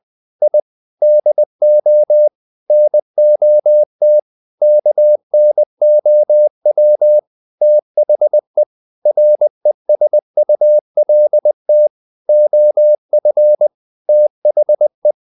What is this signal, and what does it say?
Morse code, heard as a keyed tone, about 20 wpm, 600 Hz. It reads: I DO NOT KNOW THE RESULT OF THE
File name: the old man and the sea / the old man and the sea part 1 20wpm